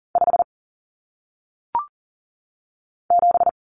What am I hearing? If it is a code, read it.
Morse code, heard as a keyed tone, about 40 wpm, 700 Hz. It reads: 5 7
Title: Common Cut Numbers - 1-Digit 40wpm